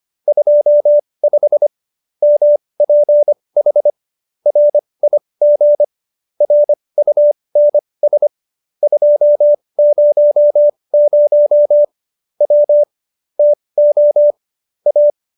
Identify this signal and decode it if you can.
Morse code, heard as a keyed tone, about 25 wpm, spaced as for 20 wpm, 590 Hz. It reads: 25 MPH RIG RUNS 200 W TO A